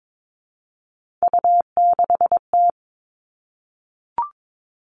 Morse code, heard as a keyed tone, about 22 wpm, 700 Hz. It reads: U6T
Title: Three Challenging Letter-Number Combinations 22wpm